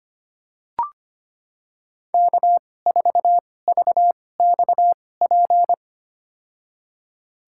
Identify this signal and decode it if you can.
Morse code, heard as a keyed tone, about 25 wpm, spaced as for 18 wpm, 700 Hz. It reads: K4VXP
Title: USA Call Signs - 5 Characters - 2x Character Spacing - 25wpm